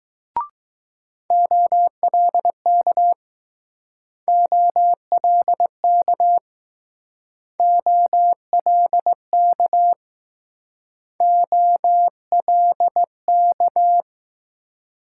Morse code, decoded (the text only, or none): OLK OLK OLK OLK